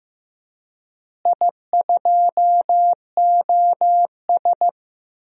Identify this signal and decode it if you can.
Morse code, heard as a keyed tone, about 15 wpm, 700 Hz. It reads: I2OS
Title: DXCC-100 Call Signs - 4 Characters - Repeated 3x - 15wpm